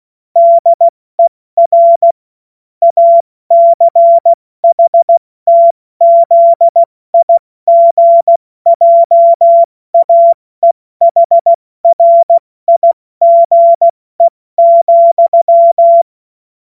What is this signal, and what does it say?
Morse code, heard as a keyed tone, about 16 wpm, spaced as for 14 wpm, 680 Hz. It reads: DER ACHTZIGJAEHRIGE,
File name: Morsekurs / Witze / Witz 04. Januar 2022, 70 F80